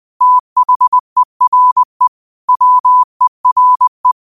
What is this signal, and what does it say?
Morse code, heard as a keyed tone, about 20 wpm, 1000 Hz. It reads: THERE WERE
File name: qotdfiles / 20wpm